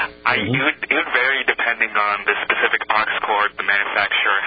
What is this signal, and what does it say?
Emergency Alert System (SAME) tones, heard as harsh buzzing bursts, with speech. It is heard from 1.0 to 4.0 s.